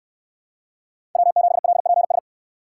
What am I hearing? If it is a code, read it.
Morse code, heard as a keyed tone, about 66 wpm, 700 Hz. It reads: W7LPS